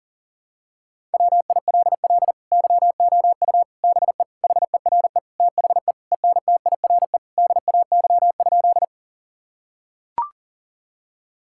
Morse code, heard as a keyed tone, about 40 wpm, 700 Hz. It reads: WILL YOU BE HERE THE ENTIRE DAY?
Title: Sentences from Top 1000 Words - Review Part 5 - 40wpm